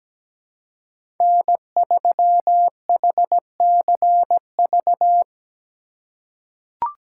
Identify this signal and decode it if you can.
Morse code, heard as a keyed tone, about 17 wpm, 700 Hz. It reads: N3HCV
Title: USA Call Signs - 5 Characters - 3x Character Spacing - 17wpm